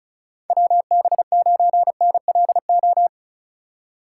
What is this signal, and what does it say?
Morse code, heard as a keyed tone, about 36 wpm, 700 Hz. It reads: WB9NLO
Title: USA Call Signs - 2x Character Spacing - 35wpm